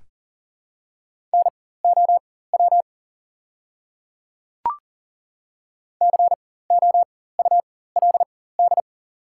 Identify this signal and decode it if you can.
Morse code, heard as a keyed tone, about 40 wpm, spaced as for 19 wpm, 690 Hz. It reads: NOW COULD
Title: 4x Character Spacing - Top 100 Words - 40wpm